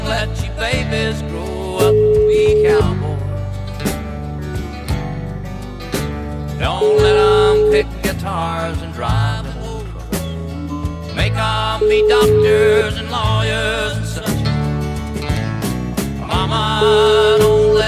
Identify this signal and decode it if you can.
Ringback tone (Europe, 425 Hz), with noise.